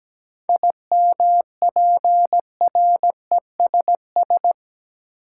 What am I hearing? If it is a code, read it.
Morse code, heard as a keyed tone, about 17 wpm, 700 Hz. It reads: IMPRESS
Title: Seven Letter Words - 17wpm